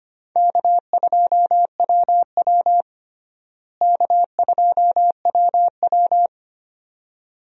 Morse code, heard as a keyed tone, about 25 wpm, 700 Hz. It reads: K2WW K2WW
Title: USA Call Signs - 4 Characters - Repeated 2x - 25wpm